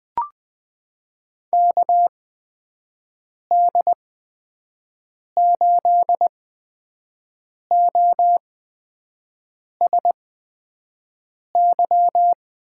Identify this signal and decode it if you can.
Morse code, heard as a keyed tone, about 20 wpm, spaced as for 5 wpm, 700 Hz. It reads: KD8OSY
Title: USA Call Signs - 6 Characters - 8x Character Spacing - 20wpm